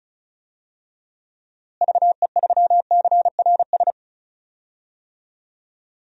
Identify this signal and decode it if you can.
Morse code, heard as a keyed tone, about 36 wpm, 700 Hz. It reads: VE3CRS
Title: DXCC-100 Call Signs - 6 Characters - Repeated 2x - 35wpm